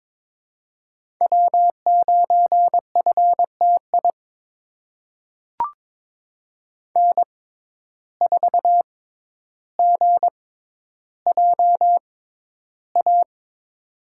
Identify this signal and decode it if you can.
Morse code, heard as a keyed tone, about 22 wpm, spaced as for 11 wpm, 700 Hz. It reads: W9FTI N4GJA
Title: USA Call Signs - 5 Characters - 6x Character Spacing - 22wpm